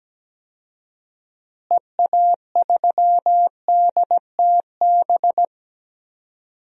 Morse code, heard as a keyed tone, about 17 wpm, 700 Hz. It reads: EA3DTB
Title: DXCC-100 Call Signs - 6 Characters - Repeated 3x - 17wpm